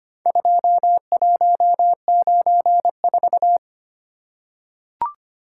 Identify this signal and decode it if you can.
Morse code, heard as a keyed tone, about 25 wpm, 700 Hz. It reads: 2194
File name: practice / 25wpm / Four Letter-Number Combinations - 25wpm